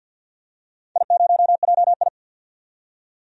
Morse code, heard as a keyed tone, about 53 wpm, 700 Hz. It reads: I0JI
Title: DXCC-100 Call Signs - 4 Characters - 50wpm